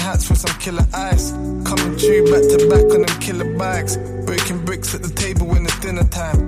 Ringback tone (Europe, 425 Hz), with noise.